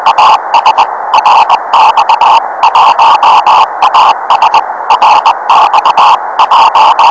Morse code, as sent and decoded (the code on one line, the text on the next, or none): none